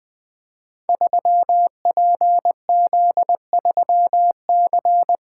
Morse code, heard as a keyed tone, about 20 wpm, 700 Hz. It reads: 3PZ3C
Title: Code Groups - 5 Letter:Numbers - 20wpm